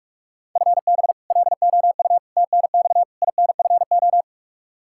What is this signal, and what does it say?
Morse code, heard as a keyed tone, about 41 wpm, 700 Hz. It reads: FB LOU TNX INFO
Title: K1USN SST QSO Practice 45wpm